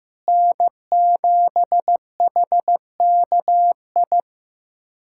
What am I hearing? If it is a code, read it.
Morse code, heard as a keyed tone, about 16 wpm, 700 Hz. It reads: N7HKI
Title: USA Callsign DE Callsign - 5 Characters - 15wpm